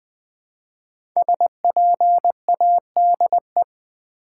Morse code, heard as a keyed tone, about 20 wpm, 700 Hz. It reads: SPADE
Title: Easy Words with Hints - 20wpm